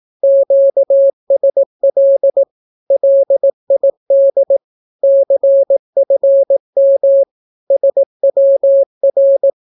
Morse code, heard as a keyed tone, about 18 wpm, 550 Hz. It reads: QSL LID CFM SWR
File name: naukapodstaw / skroty / skroty